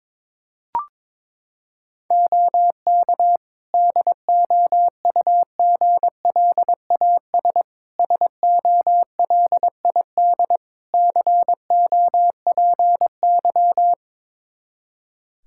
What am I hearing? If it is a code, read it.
Morse code, heard as a keyed tone, about 22 wpm, 700 Hz. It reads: OK DOUGLAS SOLID COPY
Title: Practice QSO - Exchange 3 - More Info - 22wpm